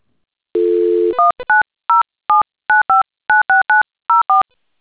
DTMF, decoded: 19079596904